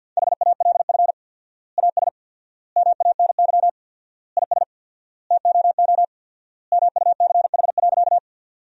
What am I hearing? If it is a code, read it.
Morse code, heard as a keyed tone, about 50 wpm, 700 Hz. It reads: HALF AS MANY IS TOO MUCH.
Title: 4x Word-Spacing - Sentences from Top 400 Words - 50wpm